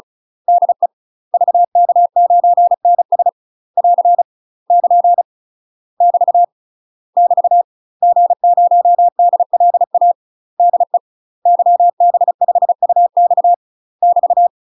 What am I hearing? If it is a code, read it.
Morse code, heard as a keyed tone, about 36 wpm, 700 Hz. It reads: DE VK9NS <AR> <KN> <BT> <BT> G0DLA DE YB5UX <BT>
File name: UK MASTER QSOs TRAINING / 35WPM / UK MASTER QSOs 0020